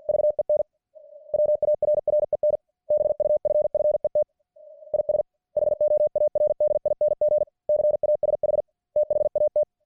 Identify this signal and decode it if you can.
Morse code, heard as a keyed tone, about 50 wpm, 600 Hz. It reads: VEN WARREN BUFFET IS HOARDING CASH THAT